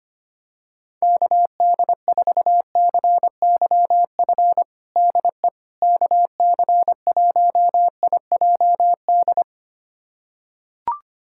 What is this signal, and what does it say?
Morse code, heard as a keyed tone, about 25 wpm, 700 Hz. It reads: KD4CYF DE KC1IJD
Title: USA Callsign DE Callsign - 6 Characters - 25wpm